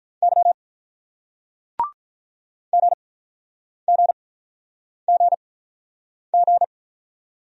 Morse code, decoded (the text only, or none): Y GGGG